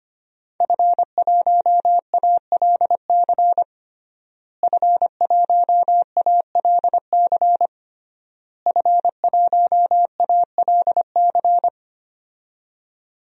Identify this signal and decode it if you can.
Morse code, heard as a keyed tone, about 25 wpm, 700 Hz. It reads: F1ALC F1ALC F1ALC
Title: DXCC-100 Call Signs - 5 Characters - Repeated 3x - 25wpm